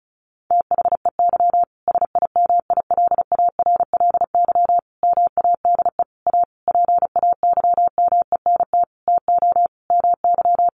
Morse code, heard as a keyed tone, about 34 wpm, 700 Hz. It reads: THEY SIMILARLY MADE A PAYMENT TO MY
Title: Sentences from Top 1400 Words 35wpm